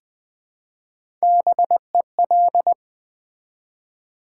Morse code, heard as a keyed tone, about 20 wpm, 700 Hz. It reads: BEL
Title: Build A Word - 1 Letter At A Time - 3 to 5-Letter Words - 20wpm